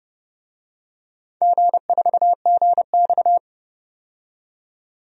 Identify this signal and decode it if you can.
Morse code, heard as a keyed tone, about 29 wpm, 700 Hz. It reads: G4GX
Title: DXCC-100 Call Signs - 4 Characters - 30wpm